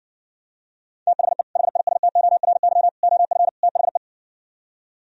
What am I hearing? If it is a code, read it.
Morse code, heard as a keyed tone, about 60 wpm, 700 Hz. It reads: THE HISTORY OF THE